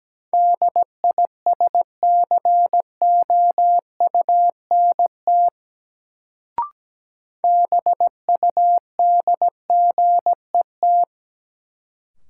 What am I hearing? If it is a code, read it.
Morse code, heard as a keyed tone, about 17 wpm, 700 Hz. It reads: DISCOUNT BUDGET
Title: Sentences from Top 1200 Words - New Word Drill - 17wpm